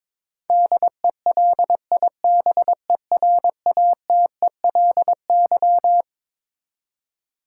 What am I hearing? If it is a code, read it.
Morse code, heard as a keyed tone, about 22 wpm, 700 Hz. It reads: DELIBERATELY
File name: Twelve Letter Words - 22wpm